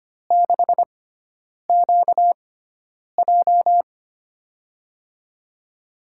Morse code, heard as a keyed tone, about 25 wpm, spaced as for 9 wpm, 700 Hz. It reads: 6QJ
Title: USA Call Signs - 4 Characters - 6x Character Spacing - 25wpm